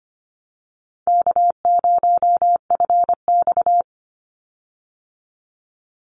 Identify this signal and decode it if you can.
Morse code, heard as a keyed tone, about 25 wpm, 700 Hz. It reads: K0FX
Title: Easy Call Signs Repeated 3x - 25wpm